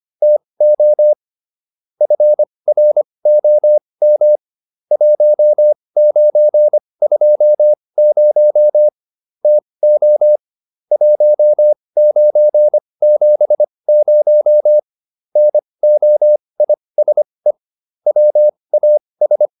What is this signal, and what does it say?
Morse code, heard as a keyed tone, about 25 wpm, spaced as for 20 wpm, 600 Hz. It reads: TO FROM 1920 TO 1970 NOISE WAS